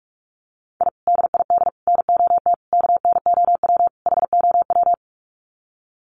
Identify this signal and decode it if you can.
Morse code, heard as a keyed tone, about 44 wpm, 700 Hz. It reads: I DID NOT KNOW HOW
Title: Sets of 5 Words Encore 45wpm